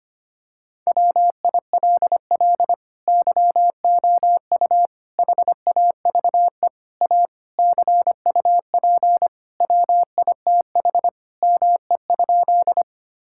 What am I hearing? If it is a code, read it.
Morse code, heard as a keyed tone, about 25 wpm, 700 Hz. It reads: WILL YOU HAVE A CUP WITH ME?
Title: Sentences from Top 1600 Words 25wpm